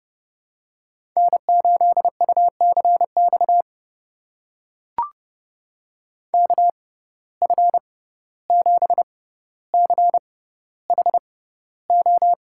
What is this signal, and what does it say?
Morse code, heard as a keyed tone, about 30 wpm, spaced as for 15 wpm, 700 Hz. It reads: N8UCX KF7CHO
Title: USA Call Signs - 6x Character Spacing - 30wpm